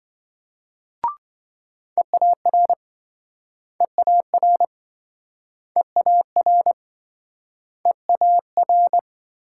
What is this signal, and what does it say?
Morse code, heard as a keyed tone, about 26 wpm, 700 Hz. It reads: EAR EAR EAR EAR